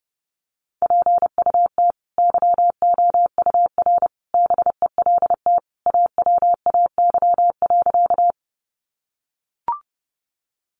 Morse code, heard as a keyed tone, about 30 wpm, 700 Hz. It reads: PUT YOUR BELT AWAY.
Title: Sentences from Top 1400 Words 30wpm